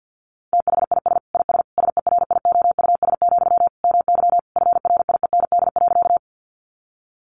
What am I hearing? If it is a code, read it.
Morse code, heard as a keyed tone, about 50 wpm, 700 Hz. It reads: THIS IS SERIOUS, MY FRIEND.
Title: Sentences from Top 1100 Words 50wpm